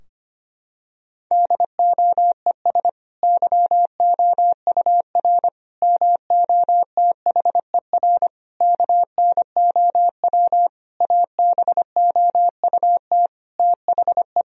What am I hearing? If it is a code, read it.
Morse code, heard as a keyed tone, about 25 wpm, 700 Hz. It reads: DOES YOUR MOTHER KNOW ABOUT THE
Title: Sentences from Top 1000 Words - Review Part 4 - 25wpm